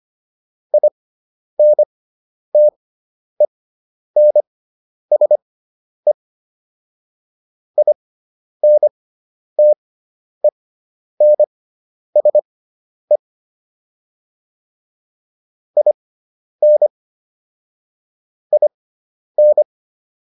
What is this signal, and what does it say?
Morse code, heard as a keyed tone, about 25 wpm, spaced as for 10 wpm, 600 Hz. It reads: INTENSE INTENSE IN IN